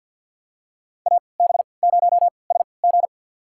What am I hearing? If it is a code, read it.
Morse code, heard as a keyed tone, about 52 wpm, spaced as for 28 wpm, 700 Hz. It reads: AB0SG
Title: USA Call Signs - 5 Characters - 3x Character Spacing - 50wpm